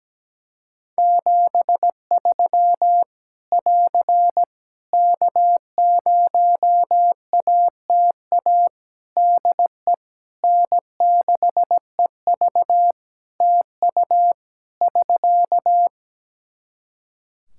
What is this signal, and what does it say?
Morse code, heard as a keyed tone, about 17 wpm, 700 Hz. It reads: 73 <AR> K0ATA DE N6EV TU <SK>
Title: Practice QSO - Exchange 4 - Ending - 17wpm